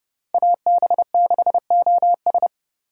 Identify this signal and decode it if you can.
Morse code, heard as a keyed tone, about 37 wpm, 700 Hz. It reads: AB6OS
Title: Speed Racing - USA Call Signs 30wpm